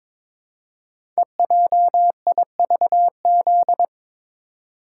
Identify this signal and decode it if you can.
Morse code, heard as a keyed tone, about 22 wpm, 700 Hz. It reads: EJIVZ
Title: Code Groups - 5 Letters - 22wpm